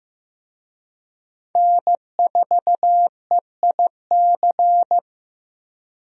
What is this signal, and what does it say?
Morse code, heard as a keyed tone, about 15 wpm, 700 Hz. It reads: N4EIC
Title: USA Call Signs - 5 Characters - 15wpm